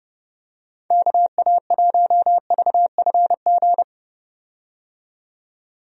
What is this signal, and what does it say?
Morse code, heard as a keyed tone, about 30 wpm, 700 Hz. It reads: KA1VFG
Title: USA Call Signs - 6 Characters - 30wpm